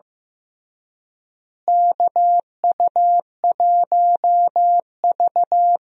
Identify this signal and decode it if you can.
Morse code, heard as a keyed tone, about 15 wpm, 700 Hz. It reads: KU1V
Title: USA Call Signs - 4 Characters - Repeated 3x - 15wpm